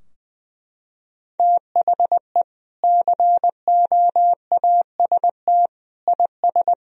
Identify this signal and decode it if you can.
Morse code, heard as a keyed tone, about 20 wpm, 700 Hz. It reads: THE COAST IS
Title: Sentences from Top 1000 Words - Review Part 4 - 20wpm